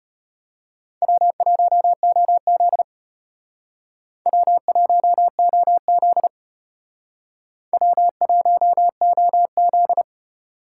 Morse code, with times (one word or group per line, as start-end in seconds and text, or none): 1.0-2.8 W1OZ
4.3-6.3 W1OZ
7.7-10.0 W1OZ